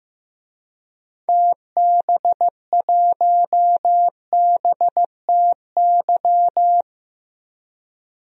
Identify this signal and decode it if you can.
Morse code, heard as a keyed tone, about 16 wpm, 700 Hz. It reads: TB1BTY
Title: DXCC-100 Call Signs - 6 Characters - 15wpm